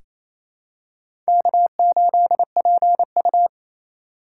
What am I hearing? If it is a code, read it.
Morse code, heard as a keyed tone, about 28 wpm, 700 Hz. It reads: K8PU
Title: USA Call Signs - 4x Character Spacing - 28wpm